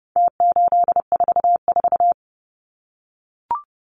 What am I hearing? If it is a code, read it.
Morse code, heard as a keyed tone, about 30 wpm, 700 Hz. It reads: T844
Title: Numbers - Contest Style - 4-Digit 30wpm